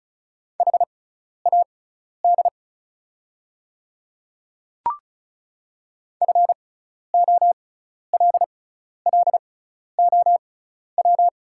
Morse code, heard as a keyed tone, about 36 wpm, spaced as for 12 wpm, 700 Hz. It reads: HAD FOLLOW
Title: 6x Character Spacing - Top 200 Words - 35wpm